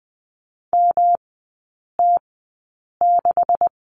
Morse code, heard as a keyed tone, about 19 wpm, spaced as for 8 wpm, 700 Hz. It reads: MT6
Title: 166 Lesson 27e - Number 7 - Three Characters 20wpm